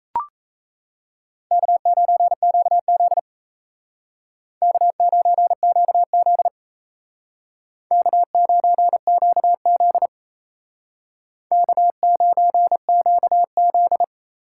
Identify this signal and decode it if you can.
Morse code, heard as a keyed tone, about 36 wpm, 700 Hz. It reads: K9QZ K9QZ K9QZ K9QZ